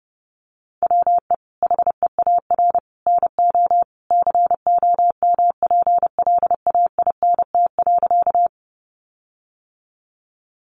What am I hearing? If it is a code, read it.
Morse code, heard as a keyed tone, about 30 wpm, 700 Hz. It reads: WE HEAR NO COMPLAINT.
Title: Sentences from Top 1400 Words 30wpm